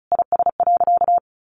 Morse code, heard as a keyed tone, about 34 wpm, 700 Hz. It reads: IS.
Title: Sentences from Top 200 Words 35wpm